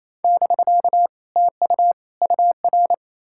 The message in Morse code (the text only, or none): <BK> TU UR